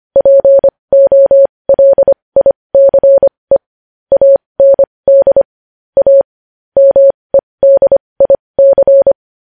Morse code, heard as a keyed tone, about 25 wpm, spaced as for 20 wpm, 550 Hz. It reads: POLICE AND A MEDIC
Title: News Headlines at 20